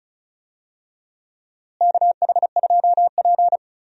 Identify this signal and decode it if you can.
Morse code, heard as a keyed tone, about 36 wpm, 700 Hz. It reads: KH2P
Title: USA Call Signs - 4 Characters - Repeated 2x - 35wpm